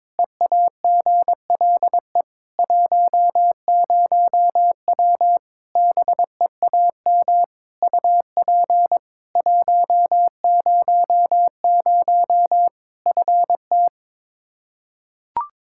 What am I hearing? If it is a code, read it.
Morse code, heard as a keyed tone, about 21 wpm, 700 Hz. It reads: EAGLE 10W BEAM UP 100 FT
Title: Practice QSO - Exchange 2 - Info - 22wpm